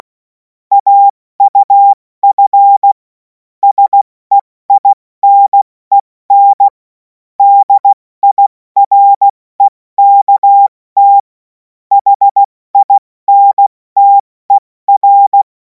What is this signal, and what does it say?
Morse code, heard as a keyed tone, about 16 wpm, 800 Hz. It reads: AUF SEINEN DIREKT HINTER
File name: Morsekurs / Witze / Witz 17. Februar 2026 70 F80